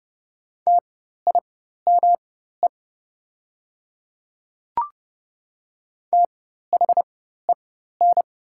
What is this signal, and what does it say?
Morse code, heard as a keyed tone, about 30 wpm, spaced as for 14 wpm, 700 Hz. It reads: TIME THEN
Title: 4x Character Spacing - Top 100 Words - 30wpm